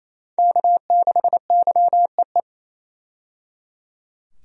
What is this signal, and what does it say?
Morse code, heard as a keyed tone, about 28 wpm, 700 Hz. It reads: K6YEE
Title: USA Call Signs - 5 Characters - 28wpm